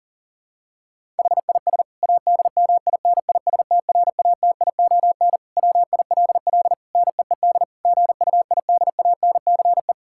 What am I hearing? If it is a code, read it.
Morse code, heard as a keyed tone, about 37 wpm, 700 Hz. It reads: HIS ADMINISTRATION WILL NEED GUIDANCE
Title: Sentences from Top 1600 Words 40wpm